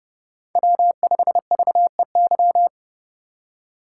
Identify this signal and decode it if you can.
Morse code, heard as a keyed tone, about 29 wpm, 700 Hz. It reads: W5VEY